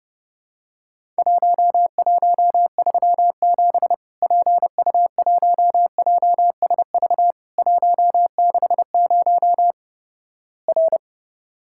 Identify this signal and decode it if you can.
Morse code, heard as a keyed tone, about 30 wpm, 700 Hz. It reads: 1137 PU1JSV 160 R
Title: Worked All Europe DX Contest - QTC Exchange - 30wpm